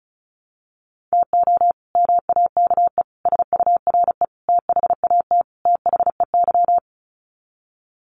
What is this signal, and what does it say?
Morse code, heard as a keyed tone, about 34 wpm, 700 Hz. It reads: TO MAKE SURE THAT THEY
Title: Sets of 5 Words Encore2 35wpm